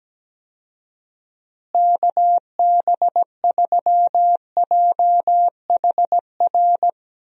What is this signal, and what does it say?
Morse code, heard as a keyed tone, about 17 wpm, 700 Hz. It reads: KB3JHR
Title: USA Call Signs - 6 Characters - Repeated 2x - 17wpm